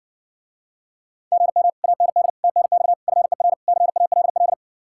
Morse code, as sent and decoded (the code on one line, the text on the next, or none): --. -- / .- -. -.. / - -. -..- / ..-. .-. / -.-. .- .-.. .-..
GM AND TNX FER CALL